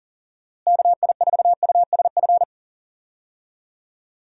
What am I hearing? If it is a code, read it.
Morse code, heard as a keyed tone, about 39 wpm, 700 Hz. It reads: KI4USF
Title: USA Call Signs - 6 Characters - Repeated 2x - 40wpm